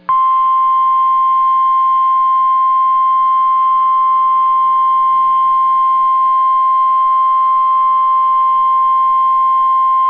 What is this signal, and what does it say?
Emergency Alert System (SAME) tones (the 1050 Hz attention tone). It is heard from 0.4 to 9.8 s.